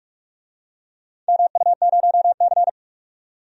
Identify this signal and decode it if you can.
Morse code, heard as a keyed tone, about 43 wpm, 700 Hz. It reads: MU0C
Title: DXCC-100 Call Signs - 4 Characters - 45wpm